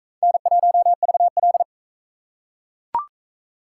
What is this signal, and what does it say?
Morse code, heard as a keyed tone, about 41 wpm, 700 Hz. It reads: N1VL